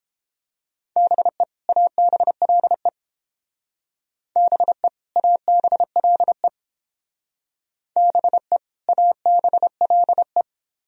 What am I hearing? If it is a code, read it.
Morse code, heard as a keyed tone, about 28 wpm, 700 Hz. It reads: BE ABLE BE ABLE BE ABLE